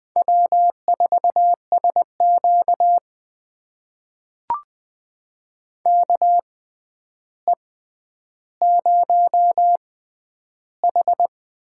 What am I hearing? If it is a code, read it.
Morse code, heard as a keyed tone, about 20 wpm, spaced as for 10 wpm, 700 Hz. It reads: W4SQ KE0H
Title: USA Call Signs - 6x Character Spacing - 20wpm